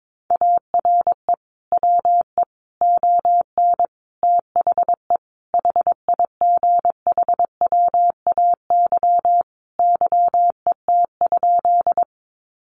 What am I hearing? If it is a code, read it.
Morse code, heard as a keyed tone, about 22 wpm, 700 Hz. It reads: ARE WE ON THE HIGHWAY YET?
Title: Sentences from Top 1400 Words 22wpm